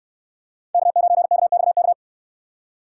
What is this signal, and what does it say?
Morse code, heard as a keyed tone, about 69 wpm, 700 Hz. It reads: K9KXK